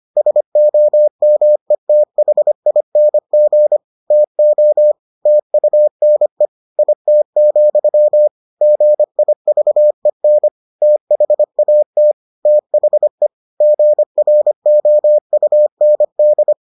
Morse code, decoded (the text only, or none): SOMETHING TO TUNE IT, GIVEN THAT THE GROUND